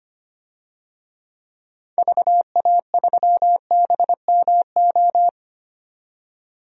Morse code, heard as a keyed tone, about 25 wpm, 700 Hz. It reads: VA3BMO
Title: DXCC-100 Call Signs - 6 Characters - 25wpm